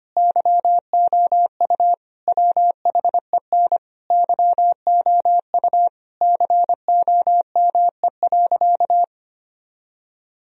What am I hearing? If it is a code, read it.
Morse code, heard as a keyed tone, about 25 wpm, 700 Hz. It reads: YOU WHEN YOU COME.
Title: Sentences from Top 1000 Words - Review Part 4 - 25wpm